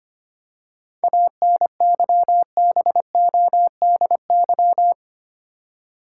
Morse code, heard as a keyed tone, about 25 wpm, 700 Hz. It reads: ANYBODY
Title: Seven Letter Words - 25wpm